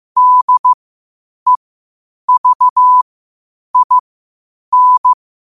Morse code, heard as a keyed tone, about 15 wpm, spaced as for 8 wpm, 1000 Hz. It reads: DEVIN
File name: qotdfiles / 8wpm